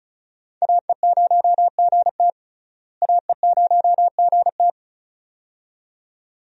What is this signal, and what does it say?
Morse code, heard as a keyed tone, about 36 wpm, 700 Hz. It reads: AE0GT AE0GT
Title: USA Call Signs - 5 Characters - Repeated 2x - 35wpm